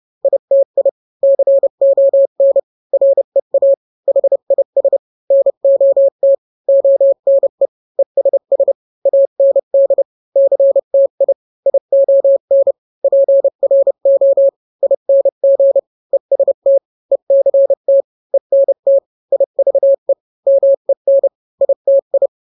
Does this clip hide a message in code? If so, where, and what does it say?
Yes: Morse code from 0.2 to 22.3 s, heard as a keyed tone, about 30 wpm, 550 Hz. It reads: ITI CON REA HIS NOT ONE ESS AND CTI ION PRO ING EST ECT ENT IVE MEN ITI